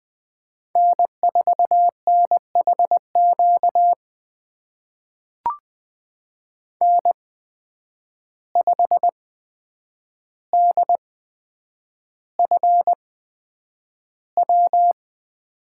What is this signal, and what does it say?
Morse code, heard as a keyed tone, about 20 wpm, spaced as for 9 wpm, 700 Hz. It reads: N4NHQ N5DFW
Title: USA Call Signs - 8x Character Spacing - 20wpm